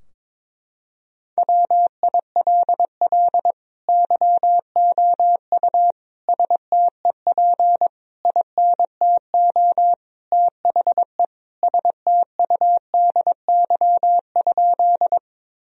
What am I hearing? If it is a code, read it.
Morse code, heard as a keyed tone, about 21 wpm, 700 Hz. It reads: WILL YOU STEP INTO THE STUDY?
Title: Sentences from Top 1000 Words - Review Part 2 - 22wpm